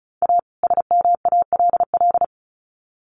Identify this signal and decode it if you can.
Morse code, heard as a keyed tone, about 34 wpm, 700 Hz. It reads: A SMALL
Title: Sets of 2 Words 35wpm